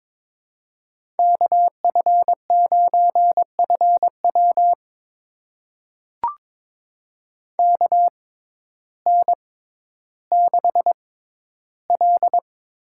Morse code, heard as a keyed tone, about 21 wpm, spaced as for 12 wpm, 700 Hz. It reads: KF9FW KN6L
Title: USA Call Signs - 5 Characters - 6x Character Spacing - 22wpm